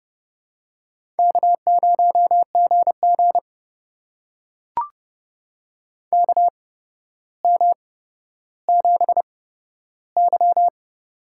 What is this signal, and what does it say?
Morse code, heard as a keyed tone, about 30 wpm, spaced as for 13 wpm, 700 Hz. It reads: K0GG KM7Y
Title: USA Call Signs - 4 Characters - 8x Character Spacing - 30wpm